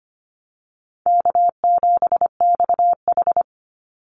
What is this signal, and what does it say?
Morse code, heard as a keyed tone, about 25 wpm, 700 Hz. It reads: K7XH
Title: Call Signs - 25wpm